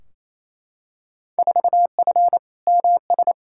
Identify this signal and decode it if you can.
Morse code, heard as a keyed tone, about 28 wpm, 700 Hz. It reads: 4F MS
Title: Field Day Exchanges Encore - 28wpm